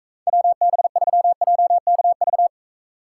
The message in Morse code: WB3JKV

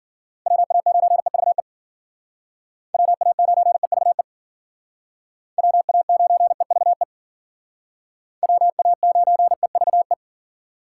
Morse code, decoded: WA9EVE WA9EVE WA9EVE WA9EVE

